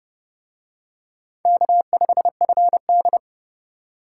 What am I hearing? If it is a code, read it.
Morse code, heard as a keyed tone, about 29 wpm, 700 Hz. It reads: K5FD